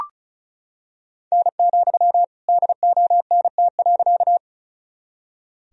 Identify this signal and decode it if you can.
Morse code, heard as a keyed tone, about 36 wpm, 700 Hz. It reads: N, DONT.
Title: Sentences from Top 300 Words and Abbreviations 35wpm